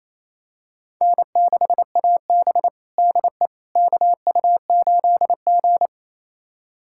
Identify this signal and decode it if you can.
Morse code, heard as a keyed tone, about 28 wpm, 700 Hz. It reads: N6AB DE KU8G
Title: USA Callsign DE Callsign - 4 Characters - 28wpm